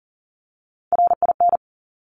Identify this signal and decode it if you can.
Morse code, heard as a keyed tone, about 41 wpm, 700 Hz. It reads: RIN
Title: Top Three Letter Combinations in Words 40wpm